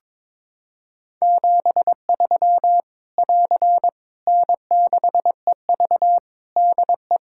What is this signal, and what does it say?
Morse code, heard as a keyed tone, about 22 wpm, 700 Hz. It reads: 73 <AR> N6EV DE
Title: K1USN SST QSO Practice 22wpm